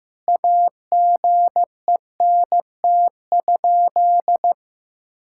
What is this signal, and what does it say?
Morse code, heard as a keyed tone, about 15 wpm, 700 Hz. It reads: AGENT?
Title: Sentences from Top 1600 Words 15wpm